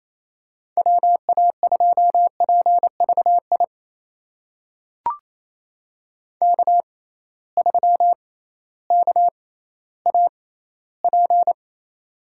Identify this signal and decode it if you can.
Morse code, heard as a keyed tone, about 28 wpm, spaced as for 15 wpm, 700 Hz. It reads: WA2PVI K3KAP
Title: USA Call Signs - 6x Character Spacing - 28wpm